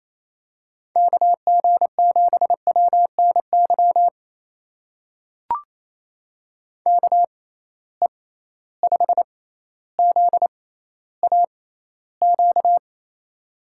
Morse code, heard as a keyed tone, about 28 wpm, spaced as for 14 wpm, 700 Hz. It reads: KG7WNY KE5ZAQ
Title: USA Call Signs - 6 Characters - 6x Character Spacing - 28wpm